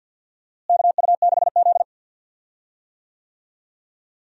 Morse code, heard as a keyed tone, about 49 wpm, 700 Hz. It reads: KU6Z
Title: USA Call Signs - 4 Characters - Repeated 3x - 50wpm